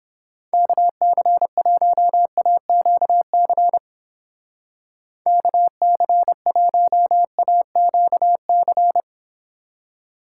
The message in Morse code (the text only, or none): KC1AQC KC1AQC